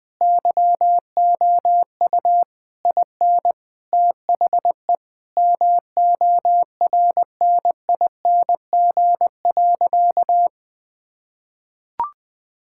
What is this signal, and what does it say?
Morse code, heard as a keyed tone, about 20 wpm, 700 Hz. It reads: YOU IN THE MORNING.
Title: Sentences with 3-Word Phrases - Spoken Beginning - 20wpm